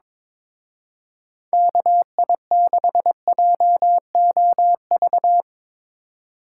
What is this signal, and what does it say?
Morse code, heard as a keyed tone, about 24 wpm, 700 Hz. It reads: KI6JOV
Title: Speed Racing - USA Call Signs 22wpm